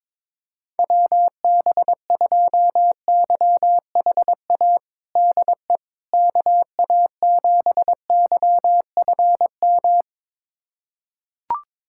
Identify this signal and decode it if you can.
Morse code, heard as a keyed tone, about 22 wpm, 700 Hz. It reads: WB2YHA DE KA7YFM
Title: USA Callsign DE Callsign - 6 Characters - 22wpm